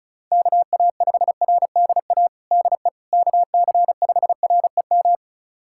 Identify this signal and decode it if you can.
Morse code, heard as a keyed tone, about 36 wpm, 700 Hz. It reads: KA5RDA DE KC5REM
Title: USA Callsign DE Callsign 35wpm